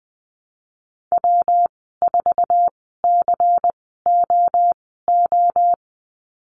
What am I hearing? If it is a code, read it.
Morse code, heard as a keyed tone, about 20 wpm, spaced as for 14 wpm, 700 Hz. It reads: W4COO
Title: 122 Lesson 20g - Number 3 - Call Signs 20wpm